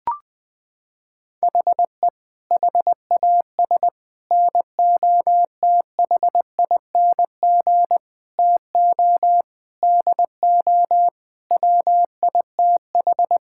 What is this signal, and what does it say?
Morse code, heard as a keyed tone, about 21 wpm, 700 Hz. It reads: HE HAS NOTHING TO DO WITH
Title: Sentences from Top 1000 Words - Review Part 5 - 20wpm